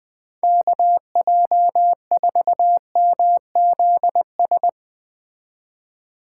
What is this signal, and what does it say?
Morse code, heard as a keyed tone, about 20 wpm, 700 Hz. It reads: KJ4MZS
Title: Speed Racing - USA Call Signs 17wpm